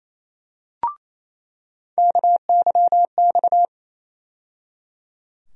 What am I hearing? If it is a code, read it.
Morse code, heard as a keyed tone, about 28 wpm, 700 Hz. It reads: KYX
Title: USA Call Sign Postfixes - 3 Letters 28wpm